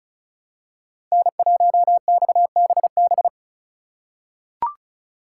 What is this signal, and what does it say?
Morse code, heard as a keyed tone, about 36 wpm, 700 Hz. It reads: N1XBB
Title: USA Call Signs - 5 Characters - 4x Character Spacing - 35wpm